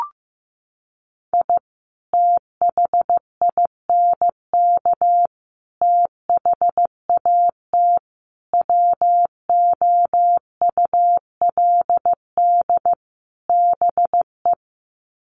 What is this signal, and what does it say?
Morse code, heard as a keyed tone, about 15 wpm, 700 Hz. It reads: I THINK THAT WOULD BE
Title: Sets of 5 Words Encore2 15wpm